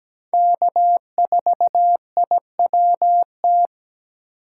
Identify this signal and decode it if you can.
Morse code, heard as a keyed tone, about 20 wpm, 700 Hz. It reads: K4IWT
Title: Speed Racing - USA 1x3 Call Signs 17wpm